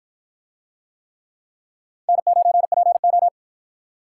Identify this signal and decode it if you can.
Morse code, heard as a keyed tone, about 53 wpm, 700 Hz. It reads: N9PO